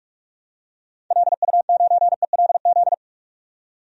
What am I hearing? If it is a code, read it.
Morse code, heard as a keyed tone, about 45 wpm, 700 Hz. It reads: LU9ELZ
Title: DXCC-100 Call Signs - 6 Characters - 45wpm